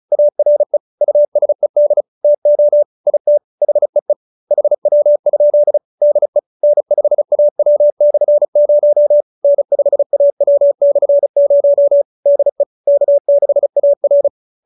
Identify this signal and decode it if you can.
Morse code, heard as a keyed tone, about 34 wpm, 590 Hz. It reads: ARE USED TO IT HEE HW? DE N5AW/0 N5AW/0 DE K6AR
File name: QSO312 35